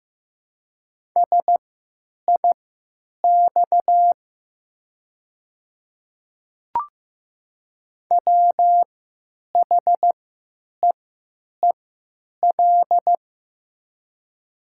Morse code, heard as a keyed tone, about 15 wpm, spaced as for 9 wpm, 700 Hz. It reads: SIX WHEEL
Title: 3x Character Spacing - Top 500 Words - 15wpm